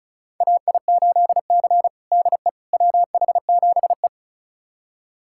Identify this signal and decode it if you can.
Morse code, heard as a keyed tone, about 36 wpm, 700 Hz. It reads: AI8C DE WH7E
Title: USA Callsign DE Callsign - 4 Characters - 35wpm